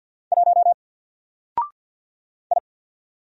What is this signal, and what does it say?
Morse code, heard as a keyed tone, about 48 wpm, 700 Hz. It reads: 1 I
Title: Single Character (Letter:Number:Punctuation:Prosign) Alt 50wpm